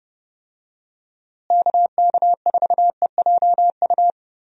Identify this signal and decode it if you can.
Morse code, heard as a keyed tone, about 32 wpm, 700 Hz. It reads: KK4EJU
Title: USA Call Signs - 6 Characters - Repeated 3x - 30wpm